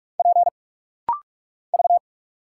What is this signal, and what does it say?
Morse code, heard as a keyed tone, about 44 wpm, 700 Hz. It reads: P V
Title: Single Letter-Number Alt - Rapid-Fire 45wpm